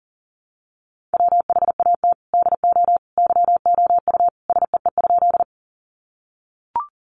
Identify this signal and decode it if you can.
Morse code, heard as a keyed tone, about 40 wpm, 700 Hz. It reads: WHAT DO YOU SEE?
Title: Sentences from Top 100 Words 40wpm